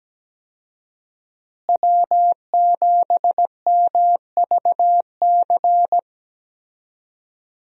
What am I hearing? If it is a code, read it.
Morse code, heard as a keyed tone, about 17 wpm, 700 Hz. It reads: W7MVC
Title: USA Call Signs - 5 Characters - Repeated 2x - 17wpm